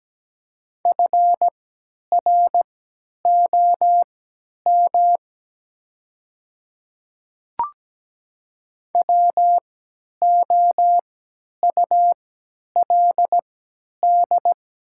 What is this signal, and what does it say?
Morse code, heard as a keyed tone, about 17 wpm, spaced as for 10 wpm, 700 Hz. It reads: FROM WOULD
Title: 3x Character Spacing - Top 200 Words - 17wpm